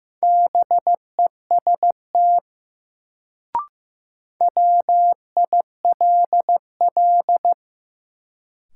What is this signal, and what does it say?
Morse code, heard as a keyed tone, about 15 wpm, 700 Hz. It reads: BEST WILL
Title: Sets of 2 Words - New Word Drill - 15wpm